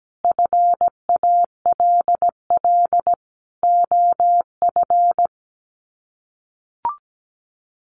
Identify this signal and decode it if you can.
Morse code, heard as a keyed tone, about 17 wpm, 700 Hz. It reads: FALL OF
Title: Sets of 5 Words Encore2 17wpm